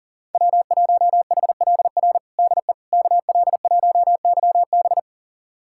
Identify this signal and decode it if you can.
Morse code, heard as a keyed tone, about 37 wpm, 700 Hz. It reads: W1HLR DE KL1YB
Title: USA Callsign DE Callsign - 5 Characters - 40wpm